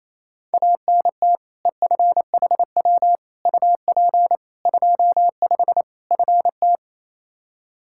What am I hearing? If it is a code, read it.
Morse code, heard as a keyed tone, about 28 wpm, 700 Hz. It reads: ANT EFHW UP 25 FT
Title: Practice QSO - Exchange 2 - Info - 28wpm